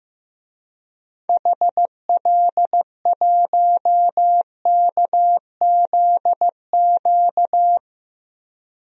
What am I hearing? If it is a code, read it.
Morse code, heard as a keyed tone, about 15 wpm, 700 Hz. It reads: HL1KZQ
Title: DXCC-100 Call Signs - 6 Characters - Repeated 2x - 15wpm